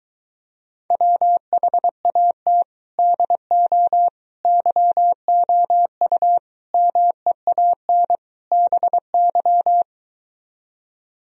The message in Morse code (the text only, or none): WHAT DO YOU MEAN BY